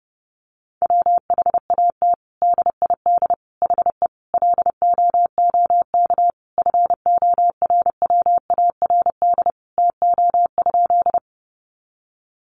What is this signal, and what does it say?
Morse code, heard as a keyed tone, about 30 wpm, 700 Hz. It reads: WHAT DID HE LOOK FORWARD TO?
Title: Sentences from Top 1300 Words 30wpm